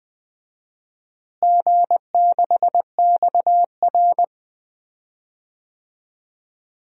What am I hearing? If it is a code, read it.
Morse code, heard as a keyed tone, about 20 wpm, 700 Hz. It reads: G6XR
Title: DXCC-100 Call Signs - 4 Characters - Repeated 2x - 20wpm